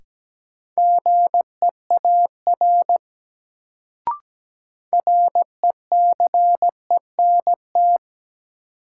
Morse code, heard as a keyed tone, about 17 wpm, 700 Hz. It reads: GEAR RECENT
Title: Sentences from Top 1400 Words - New Word Drill Alt - 17wpm